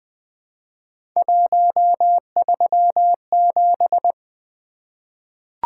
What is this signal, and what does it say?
Morse code, heard as a keyed tone, about 20 wpm, 700 Hz. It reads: 137
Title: Three Letter-Number Combinations - 20wpm